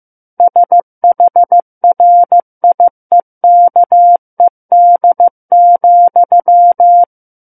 Morse code, heard as a keyed tone, about 15 wpm, 700 Hz. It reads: SHRIEKED,
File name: Book One / 06-Chapter 6 - 15 WPM